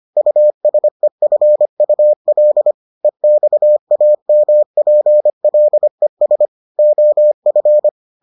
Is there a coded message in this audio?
Yes: Morse code from 0.2 to 7.9 s, heard as a keyed tone, about 25 wpm, 600 Hz. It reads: USEFUL EXAMPLES OF